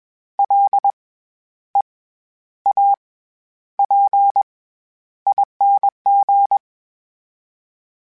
Morse code, heard as a keyed tone, about 21 wpm, spaced as for 10 wpm, 800 Hz. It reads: LEAPING
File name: Special INGs-mp3s / Special INGs-mp3s / SlowToFaster ing @20